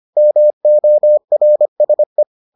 Morse code, heard as a keyed tone, about 25 wpm, 600 Hz. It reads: MORSE